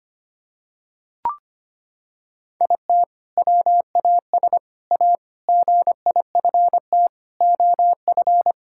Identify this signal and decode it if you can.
Morse code, heard as a keyed tone, about 25 wpm, 700 Hz. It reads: IT WAS A GIFT OF
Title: Sentences from Top 1000 Words - Review Part 6 - 25wpm